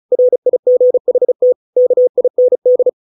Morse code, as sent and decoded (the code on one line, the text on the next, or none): .-. .. --. .... - / -.- .. -. -..
RIGHT KIND